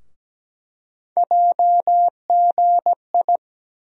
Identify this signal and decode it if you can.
Morse code, heard as a keyed tone, about 17 wpm, 700 Hz. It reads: JGI
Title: USA Call Sign Postfixes - 3 Letters 17wpm